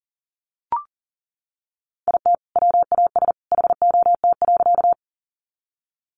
Morse code, heard as a keyed tone, about 40 wpm, 700 Hz. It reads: IT WAS HOT.
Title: Sentences from Top 500 Words 40wpm